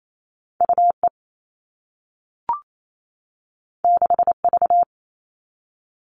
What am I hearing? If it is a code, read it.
Morse code, heard as a keyed tone, about 28 wpm, 700 Hz. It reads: UE 6V
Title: All Cut Numbers - 2-Digit 28wpm